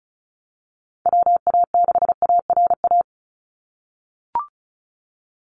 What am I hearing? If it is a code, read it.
Morse code, heard as a keyed tone, about 36 wpm, 700 Hz. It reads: WA6ARA
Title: NA SOTA Chaser Callsigns 35wpm